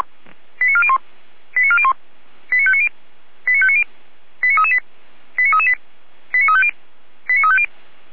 Five-tone selective call (ZVEI), heard as quick runs of five tones, tones of about 70 ms. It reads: 89401 89401 89590 89590 89338 89338 89350 89350
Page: /455/SDR/ZVEI/89700/